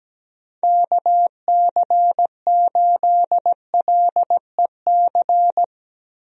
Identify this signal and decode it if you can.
Morse code, heard as a keyed tone, about 17 wpm, 700 Hz. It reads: KC8LEC
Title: USA Call Signs - 6 Characters - Repeated 3x - 17wpm